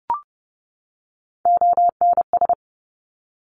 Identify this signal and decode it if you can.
Morse code, heard as a keyed tone, about 30 wpm, 700 Hz. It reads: ONS
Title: Top Three Letter Combinations in Words 30wpm